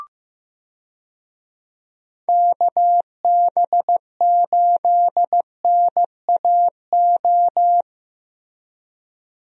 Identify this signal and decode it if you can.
Morse code, heard as a keyed tone, about 16 wpm, 700 Hz. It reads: KB8NAO
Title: USA Call Signs - 6 Characters - Repeated 2x - 15wpm